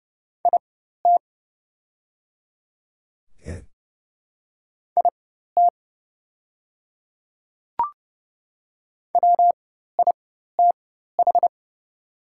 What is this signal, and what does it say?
Morse code, heard as a keyed tone, about 30 wpm, spaced as for 14 wpm, 700 Hz. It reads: IT IT WITH
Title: 4x Character Spacing - Top 200 Words - 30wpm